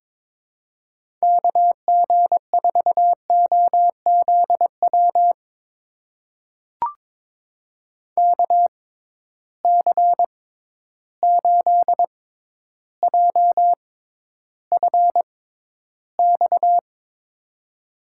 Morse code, heard as a keyed tone, about 22 wpm, spaced as for 11 wpm, 700 Hz. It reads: KG4OZW KC8JFX
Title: USA Call Signs - 6 Characters - 6x Character Spacing - 22wpm